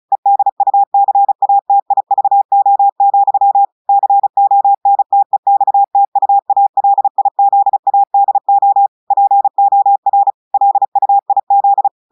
Morse code, heard as a keyed tone, about 35 wpm, 800 Hz. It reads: EDUCATIVO, CONTEXTUALIZADO POR LUIZ
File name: audios / portugais / portugais texte en clair 1 vitesse 35 mots minutes